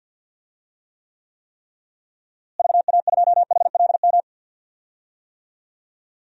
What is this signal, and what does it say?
Morse code, heard as a keyed tone, about 50 wpm, 700 Hz. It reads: VA2HLM
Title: DXCC-100 Call Signs - 6 Characters - Repeated 3x - 50wpm